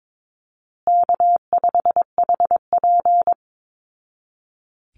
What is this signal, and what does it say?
Morse code, heard as a keyed tone, about 22 wpm, 700 Hz. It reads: K5HP
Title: Easy Call Signs Repeated 3x - 22wpm